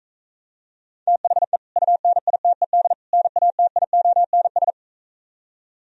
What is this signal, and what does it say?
Morse code, heard as a keyed tone, about 43 wpm, 700 Hz. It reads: THE UNITED NATIONS